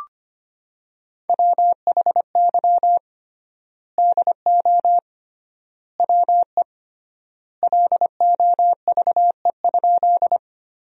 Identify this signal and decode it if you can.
Morse code, heard as a keyed tone, about 25 wpm, 700 Hz. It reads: WHY DO WE LOVE?
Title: 3x Word-Spacing - Sentences from Top 400 Words - 25wpm